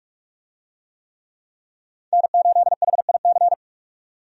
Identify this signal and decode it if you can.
Morse code, heard as a keyed tone, about 46 wpm, 700 Hz. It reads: N8HIC
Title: USA Call Signs - 5 Characters - 45wpm